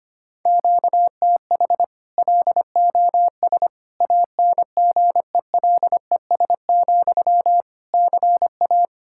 Morse code, heard as a keyed tone, about 25 wpm, 700 Hz. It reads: QTH LOS ANGELES, CA
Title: Practice QSO - Exchange 1 - Beginning - 25wpm